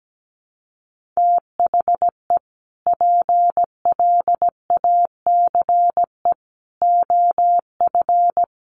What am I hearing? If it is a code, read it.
Morse code, heard as a keyed tone, about 17 wpm, 700 Hz. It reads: THE PLACE OF
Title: Sentences from Top 1500 Words 17wpm